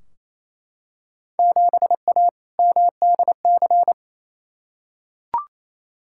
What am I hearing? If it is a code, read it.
Morse code, heard as a keyed tone, about 28 wpm, 700 Hz. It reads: 7A MDC
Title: Field Day Exchanges Encore - 28wpm